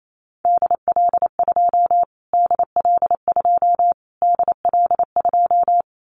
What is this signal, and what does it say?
Morse code, heard as a keyed tone, about 28 wpm, 700 Hz. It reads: DL2 DL2 DL2
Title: Callsign Prefixes for Top 25 DXCC - Repeated 3x - 28wpm